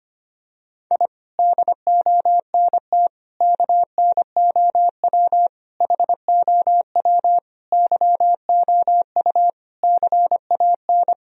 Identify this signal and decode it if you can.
Morse code, heard as a keyed tone, about 25 wpm, 700 Hz. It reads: I DONT KNOW HOW YOU CAN
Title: Sentences from Top 1000 Words - Review Part 2 - 25wpm